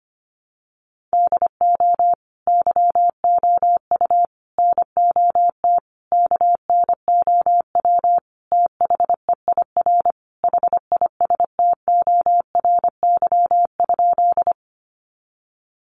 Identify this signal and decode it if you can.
Morse code, heard as a keyed tone, about 25 wpm, 700 Hz. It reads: DO YOU NOT KNOW THEIR HISTORY?
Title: Sentences from Top 600 Words 25wpm